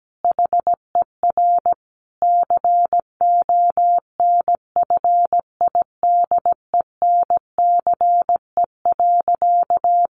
Morse code, heard as a keyed tone, about 17 wpm, 700 Hz. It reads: HER CONFIDENCE.
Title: Sentences from Top 1500 Words 17wpm